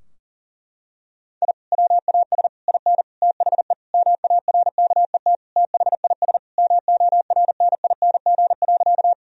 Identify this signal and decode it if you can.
Morse code, heard as a keyed tone, about 40 wpm, 700 Hz. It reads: I WAS IN THE MARKET THIS MORNING.
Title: Sentences from Top 1000 Words - Review Part 3 - 40wpm